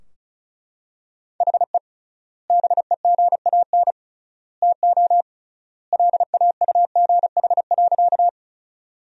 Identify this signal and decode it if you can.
Morse code, heard as a keyed tone, about 36 wpm, 700 Hz. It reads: HE BEGAN TO LAUGH.
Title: 3x Word-Spacing - Sentences from Top 500 Words - 35wpm